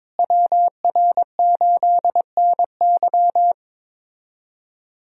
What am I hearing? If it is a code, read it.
Morse code, heard as a keyed tone, about 22 wpm, 700 Hz. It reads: WR8NY
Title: USA Call Signs - 5 Characters - Repeated 2x - 22wpm